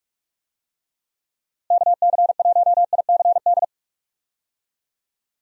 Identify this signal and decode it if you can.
Morse code, heard as a keyed tone, about 46 wpm, 700 Hz. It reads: KC1ICD
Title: USA Call Signs - 6 Characters - Repeated 2x - 45wpm